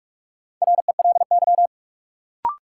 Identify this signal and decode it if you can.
Morse code, heard as a keyed tone, about 41 wpm, 700 Hz. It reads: RELY
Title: Sentences from Top 1200 Words - New Word Drill - 45wpm